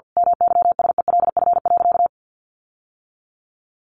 Morse code, heard as a keyed tone, about 50 wpm, 700 Hz. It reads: MYSELF.
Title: Sentences from Top 1300 Words 50wpm